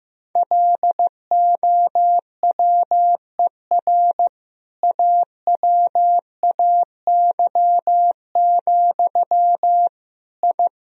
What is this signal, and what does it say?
Morse code, heard as a keyed tone, about 15 wpm, 700 Hz. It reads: LOWER AWAY, I
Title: Sentences from Top 1000 Words - Review Part 4 - 15wpm